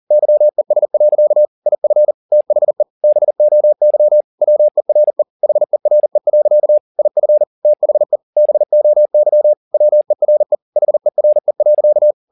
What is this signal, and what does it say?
Morse code, heard as a keyed tone, about 40 wpm, 600 Hz. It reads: YES. IF THE BOY WERE HERE. IF THE BOY WERE HERE.
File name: the old man and the sea 2 40wpm